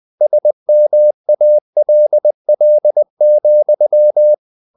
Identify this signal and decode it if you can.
Morse code, heard as a keyed tone, about 20 wpm, 600 Hz. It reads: SMALL,
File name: the old man and the sea part 2 20wpm